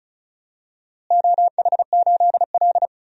Morse code, heard as a keyed tone, about 34 wpm, 700 Hz. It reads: OH8L
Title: DXCC-100 Call Signs - 4 Characters - Repeated 2x - 35wpm